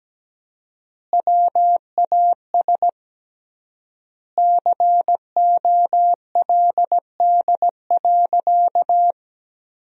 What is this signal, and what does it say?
Morse code, heard as a keyed tone, about 17 wpm, 700 Hz. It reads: WAS COLD.
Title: 3x Word-Spacing - Sentences from Top 400 Words - 17wpm